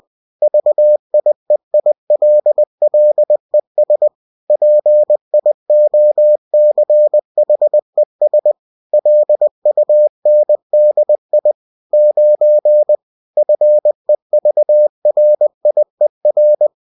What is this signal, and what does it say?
Morse code, heard as a keyed tone, about 20 wpm, 600 Hz. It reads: VIEILLES PIOCHES LUNDI 9 FEVRIER